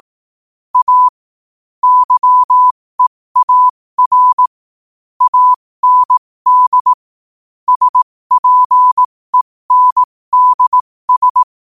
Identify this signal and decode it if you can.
Morse code, heard as a keyed tone, about 18 wpm, spaced as for 15 wpm, 1000 Hz. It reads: A YEAR AND SPENDS
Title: 15 WPM morse code quotes for Fri, 01 Mar 2024